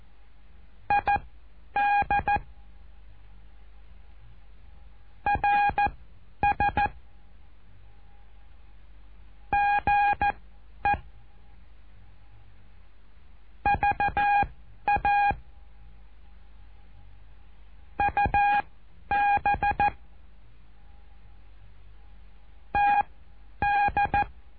Morse code, heard as a keyed tone, about 14 wpm, spaced as for 10 wpm, 820 Hz. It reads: ID RS GE VA UB TD
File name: cursos / CURSO DE TELEGRAFIA / Telegrafia - Trilha 08